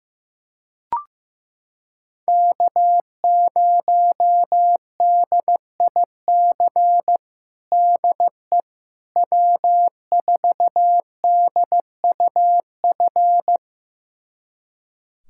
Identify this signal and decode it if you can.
Morse code, heard as a keyed tone, about 15 wpm, 700 Hz. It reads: K0DIC DE W4DUF
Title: USA Callsign DE Callsign - 5 Characters - 15wpm